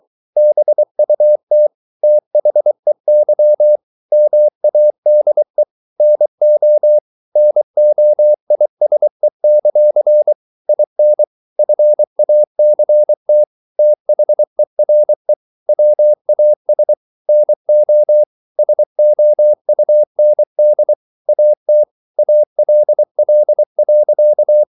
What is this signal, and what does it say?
Morse code, heard as a keyed tone, about 24 wpm, 600 Hz. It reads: BUT THEY MADE NO NOISE; IN FACT THERE WAS NO SOUND AT ALL.